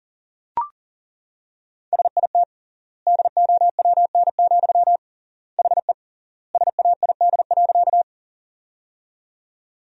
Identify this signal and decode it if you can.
Morse code, heard as a keyed tone, about 37 wpm, 700 Hz. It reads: SIT DOWN, HE SAID.
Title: 3x Word-Spacing - Sentences from Top 500 Words - 40wpm